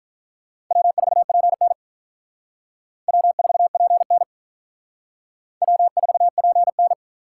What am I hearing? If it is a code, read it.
Morse code, heard as a keyed tone, about 45 wpm, 700 Hz. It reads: W4PN W4PN W4PN